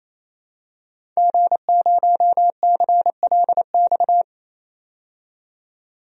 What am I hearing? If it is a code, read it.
Morse code, heard as a keyed tone, about 28 wpm, 700 Hz. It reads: G0CLX
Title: DXCC-100 Call Signs - 5 Characters - 28wpm